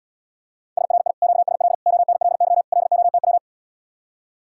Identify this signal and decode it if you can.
Morse code, heard as a keyed tone, about 80 wpm, 700 Hz. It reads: SRI QSL ONLY LOTW